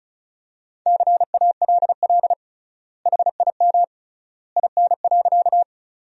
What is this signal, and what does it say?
Morse code, heard as a keyed tone, about 36 wpm, 700 Hz. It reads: CALL HIM IN.
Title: 3x Word-Spacing - Sentences from Top 100 Words - 35wpm